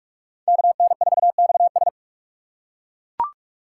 Morse code, heard as a keyed tone, about 41 wpm, 700 Hz. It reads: KN4XS
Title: USA Call Signs - 2x Character Spacing - 45wpm